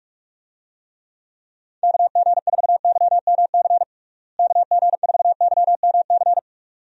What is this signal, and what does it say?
Morse code, heard as a keyed tone, about 46 wpm, 700 Hz. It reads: KG4YMC KG4YMC
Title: USA Call Signs - 6 Characters - Repeated 2x - 45wpm